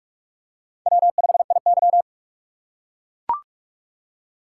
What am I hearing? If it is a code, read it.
Morse code, heard as a keyed tone, about 43 wpm, 700 Hz. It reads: W5IY
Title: USA Call Signs - 4 Characters - 2x Character Spacing - 45wpm